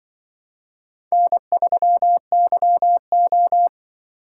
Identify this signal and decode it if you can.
Morse code, heard as a keyed tone, about 24 wpm, 700 Hz. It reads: N3YO
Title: Speed Racing - USA 1x2 Call Signs 20wpm